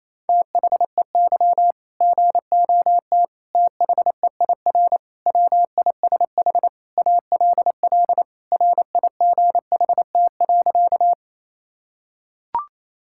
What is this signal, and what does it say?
Morse code, heard as a keyed tone, about 28 wpm, 700 Hz. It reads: THEY GOT THEIR WISH ALL RIGHT.
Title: Sentences from Top 1600 Words 28wpm